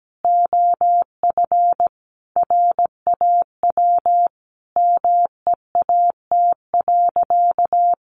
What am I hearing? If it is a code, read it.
Morse code, heard as a keyed tone, about 17 wpm, 700 Hz. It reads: OF RAW MEAT.
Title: Sentences from Top 1500 Words 17wpm